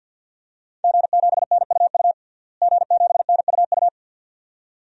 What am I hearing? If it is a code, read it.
Morse code, heard as a keyed tone, about 52 wpm, 700 Hz. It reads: G7NUU G7NUU
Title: DXCC-100 Call Signs - 5 Characters - Repeated 2x - 50wpm